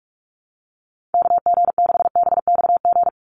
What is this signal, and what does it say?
Morse code, heard as a keyed tone, about 44 wpm, 700 Hz. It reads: KG6BXG
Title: USA Call Signs 45wpm